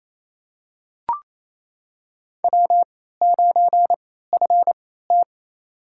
Morse code, heard as a keyed tone, about 28 wpm, spaced as for 16 wpm, 700 Hz. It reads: W9FT
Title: USA Call Signs - 4 Characters - 3x Character Spacing - 28wpm